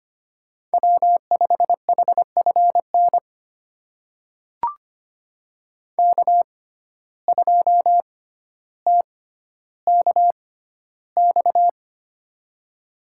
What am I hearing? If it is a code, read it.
Morse code, heard as a keyed tone, about 25 wpm, spaced as for 13 wpm, 700 Hz. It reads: W5HFN K2TKX
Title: USA Call Signs - 5 Characters - 6x Character Spacing - 25wpm